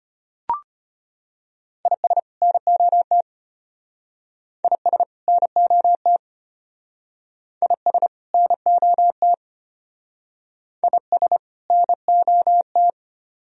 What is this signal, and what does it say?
Morse code, heard as a keyed tone, about 33 wpm, 700 Hz. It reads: IS NOT IS NOT IS NOT IS NOT